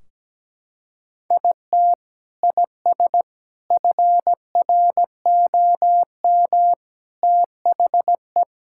Morse code, heard as a keyed tone, about 17 wpm, 700 Hz. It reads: IT IS FROM THE
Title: Sentences from Top 1600 Words 17wpm